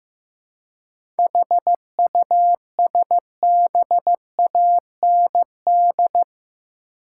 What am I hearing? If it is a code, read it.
Morse code, heard as a keyed tone, about 16 wpm, 700 Hz. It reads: HUSBAND
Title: Seven Letter Words - 15wpm